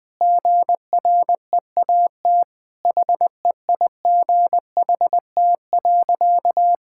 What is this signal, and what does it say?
Morse code, heard as a keyed tone, about 20 wpm, 700 Hz. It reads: GREAT HEIGHT.
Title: Sentences from Top 1600 Words 20wpm